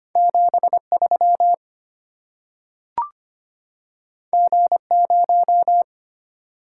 Morse code, heard as a keyed tone, about 25 wpm, 690 Hz. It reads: 73 G0
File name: Two Letter-Number Combinations - 25wpm